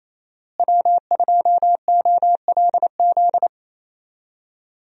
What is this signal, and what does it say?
Morse code, heard as a keyed tone, about 28 wpm, 700 Hz. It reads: W2OLZ
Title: USA Call Signs - 5 Characters - Repeated 3x - 28wpm